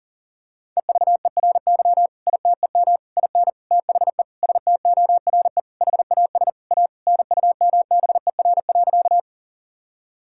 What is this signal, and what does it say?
Morse code, heard as a keyed tone, about 40 wpm, 700 Hz. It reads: EVERY ITEM IN THE STORE HAS A NUMBER.
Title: Sentences from Top 1000 Words - Review Part 6 - 40wpm